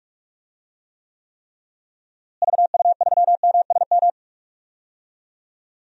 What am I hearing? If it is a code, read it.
Morse code, heard as a keyed tone, about 46 wpm, 700 Hz. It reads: VU3MSM
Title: DXCC-100 Call Signs - 6 Characters - Repeated 3x - 45wpm